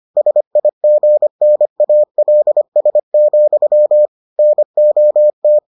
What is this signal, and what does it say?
Morse code, heard as a keyed tone, about 25 wpm, 600 Hz. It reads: SIGNALS, NOT